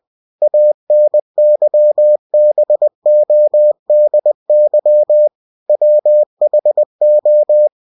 Morse code, heard as a keyed tone, about 20 wpm, 600 Hz. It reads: ANYBODY WHO
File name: cw-books / winnie-the-pooh / winnie-20wpm-page-0030